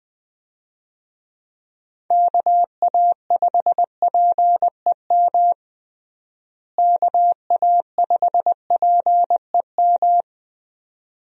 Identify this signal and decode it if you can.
Morse code, heard as a keyed tone, about 20 wpm, 700 Hz. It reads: KA5PEM KA5PEM
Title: USA Call Signs - 6 Characters - Repeated 2x - 20wpm